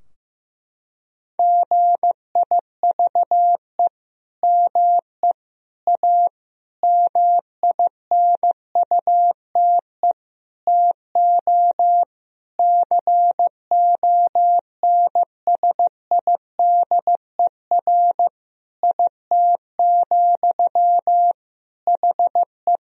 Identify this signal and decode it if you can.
Morse code, heard as a keyed tone, about 16 wpm, 700 Hz. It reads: GIVE ME A MINUTE TO CONSIDER IT, HE
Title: Sentences from Top 1000 Words - Review Part 4 - 15wpm